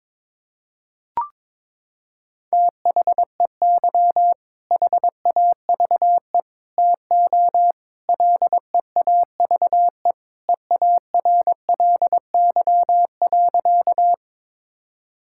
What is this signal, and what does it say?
Morse code, heard as a keyed tone, about 22 wpm, 700 Hz. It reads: THEY HAVE TO LEAVE EARLY.
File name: practice / 22wpm / Sentences with 3-Word Phrases - 22wpm